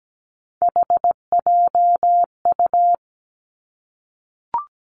Morse code, heard as a keyed tone, about 17 wpm, 700 Hz. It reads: HJU
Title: Three Challenging Letter-Number Combinations 17wpm